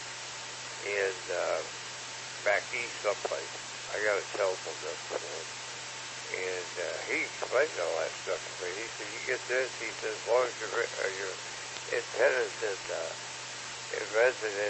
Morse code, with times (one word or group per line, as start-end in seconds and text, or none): none